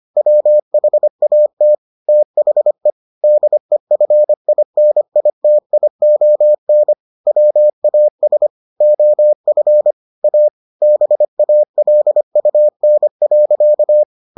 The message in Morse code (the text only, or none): WHAT THE DEFINITION WAS OF A BALUN.